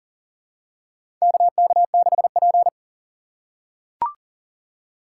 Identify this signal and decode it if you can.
Morse code, heard as a keyed tone, about 40 wpm, 700 Hz. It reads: KK6P
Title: USA Call Signs - 4 Characters - 3x Character Spacing - 40wpm